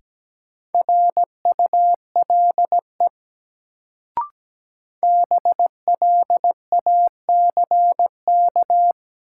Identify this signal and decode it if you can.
Morse code, heard as a keyed tone, about 17 wpm, 700 Hz. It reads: RULE BLACK
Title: Sentences from Top 400 Words - New Word Drill - 17wpm